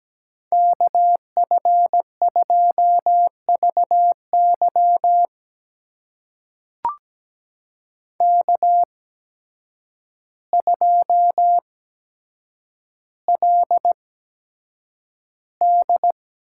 Morse code, heard as a keyed tone, about 17 wpm, spaced as for 8 wpm, 700 Hz. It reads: KF2VY K2LD
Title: USA Call Signs - 8x Character Spacing - 17wpm